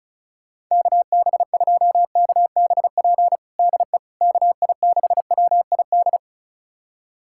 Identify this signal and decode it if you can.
Morse code, heard as a keyed tone, about 34 wpm, 700 Hz. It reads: KB2KBP DE KI6WID
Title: USA Callsign DE Callsign - 6 Characters - 35wpm